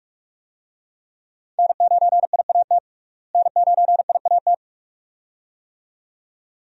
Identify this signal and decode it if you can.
Morse code, heard as a keyed tone, about 43 wpm, 700 Hz. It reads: N9IAT N9IAT
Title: USA Call Signs - 5 Characters - Repeated 2x - 45wpm